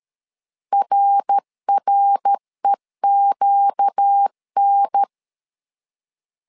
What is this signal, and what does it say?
Morse code, heard as a keyed tone, about 13 wpm, 800 Hz. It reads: RREQN
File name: LEZIONI / 4 Q 12wpm 1833